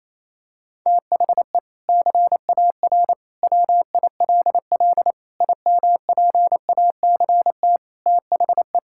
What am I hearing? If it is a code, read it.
Morse code, heard as a keyed tone, about 28 wpm, 700 Hz. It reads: THE CAR WILL IMPACT THE
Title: Sentences from Top 1000 Words - Review Part 6 - 28wpm